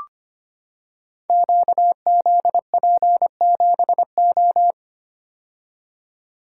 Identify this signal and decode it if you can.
Morse code, heard as a keyed tone, about 25 wpm, 700 Hz. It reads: QZP7O
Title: Code Groups - 5 Letter:Numbers - 25wpm